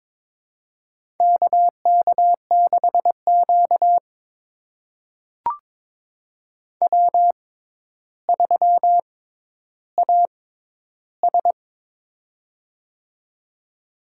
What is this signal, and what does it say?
Morse code, heard as a keyed tone, about 22 wpm, spaced as for 11 wpm, 700 Hz. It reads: KK6Q W3AS
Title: USA Call Signs - 4 Characters - 6x Character Spacing - 22wpm